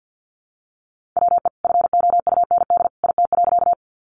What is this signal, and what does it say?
Morse code, heard as a keyed tone, about 49 wpm, 700 Hz. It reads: WE FOUND IT.
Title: Sentences from Top 300 Words 50wpm